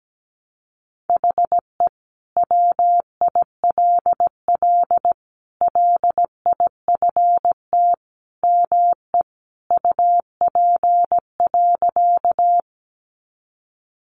Morse code, heard as a keyed tone, about 17 wpm, 700 Hz. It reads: HE WILL LIFT ME UP.
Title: Sentences from Top 1000 Words 17wpm